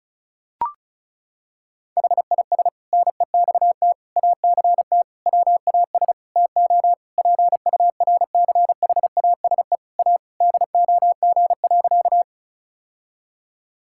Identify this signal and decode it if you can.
Morse code, heard as a keyed tone, about 36 wpm, 700 Hz. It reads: HIS NEXT ACT WAS TO PURCHASE A DOG.
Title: Sentences from Top 1000 Words - Review Part 5 - 35wpm